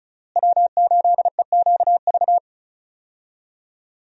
Morse code, heard as a keyed tone, about 36 wpm, 700 Hz. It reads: W8EQV
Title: USA Call Signs - 5 Characters - Repeated 3x - 35wpm